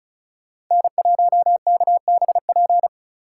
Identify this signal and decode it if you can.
Morse code, heard as a keyed tone, about 34 wpm, 700 Hz. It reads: N1KBP
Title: USA Call Signs - 3x Character Spacing - 35wpm